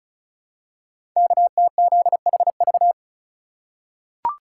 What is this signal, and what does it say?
Morse code, heard as a keyed tone, about 36 wpm, 700 Hz. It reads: KTZHV
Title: Code Groups - 5 Letters - 35wpm